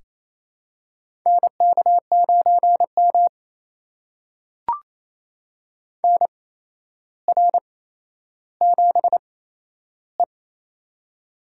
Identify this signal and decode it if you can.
Morse code, heard as a keyed tone, about 28 wpm, spaced as for 12 wpm, 700 Hz. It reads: NK9M NR7E
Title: USA Call Signs - 8x Character Spacing - 28wpm